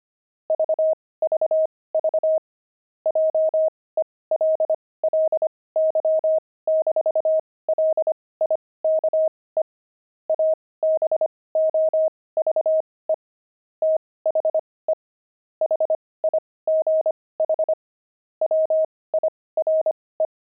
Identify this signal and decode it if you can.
Morse code, heard as a keyed tone, about 25 wpm, spaced as for 18 wpm, 630 Hz. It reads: VVV JELLY-LIKE ABOVE THE HIGH WIRE
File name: pangram-fs-18-cs-25-jelly